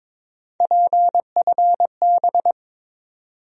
Morse code, heard as a keyed tone, about 22 wpm, 700 Hz. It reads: PFB
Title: USA Call Sign Postfixes - 3 Letters 22wpm